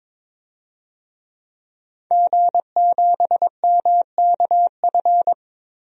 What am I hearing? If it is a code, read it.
Morse code, heard as a keyed tone, about 22 wpm, 700 Hz. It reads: G7MKF
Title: DXCC-100 Call Signs - 5 Characters - Repeated 2x - 22wpm